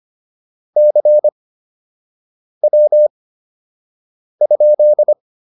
Morse code, heard as a keyed tone, about 25 wpm, spaced as for 6 wpm, 600 Hz. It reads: CW?